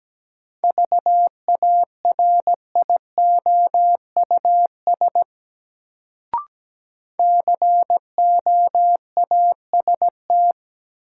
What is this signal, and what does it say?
Morse code, heard as a keyed tone, about 17 wpm, 700 Hz. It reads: VARIOUS COAST
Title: Sentences from Top 1000 Words - Review Part 4 - New Word Drill - 17wpm